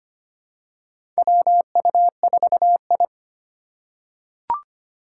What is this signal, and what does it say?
Morse code, heard as a keyed tone, about 25 wpm, 700 Hz. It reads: WU4I
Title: USA Call Signs - 4 Characters - 2x Character Spacing - 25wpm